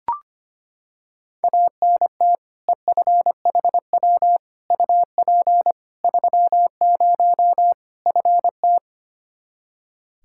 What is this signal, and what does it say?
Morse code, heard as a keyed tone, about 25 wpm, 700 Hz. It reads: ANT EFHW UP 30 FT
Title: Practice QSO - Exchange 2 - Info - 25wpm